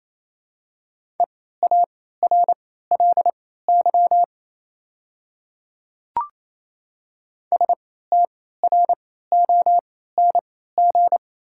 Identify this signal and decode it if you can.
Morse code, heard as a keyed tone, about 28 wpm, spaced as for 16 wpm, 690 Hz. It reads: EARLY STRONG
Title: 3x Character Spacing - Top 500 Words - 28wpm